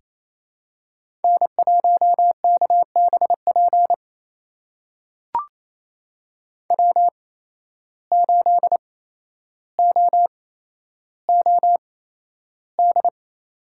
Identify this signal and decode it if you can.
Morse code, heard as a keyed tone, about 28 wpm, spaced as for 12 wpm, 700 Hz. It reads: N1KBP W8OOD
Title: USA Call Signs - 5 Characters - 8x Character Spacing - 28wpm